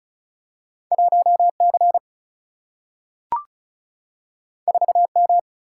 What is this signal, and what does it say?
Morse code, heard as a keyed tone, about 36 wpm, 700 Hz. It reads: 1C 4M
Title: Single-Letter Single-Number 35wpm